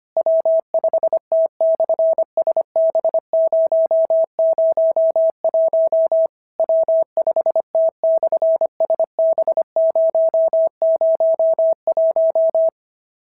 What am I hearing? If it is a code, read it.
Morse code, heard as a keyed tone, about 25 wpm, 650 Hz. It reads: W5T/SB001 W5T/SB001
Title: SOTA QSO Practice 25wpm